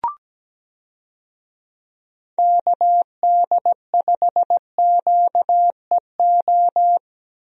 Morse code, heard as a keyed tone, about 17 wpm, 700 Hz. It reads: KD5QEO
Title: USA Call Signs - 6 Characters - Repeated 3x - 17wpm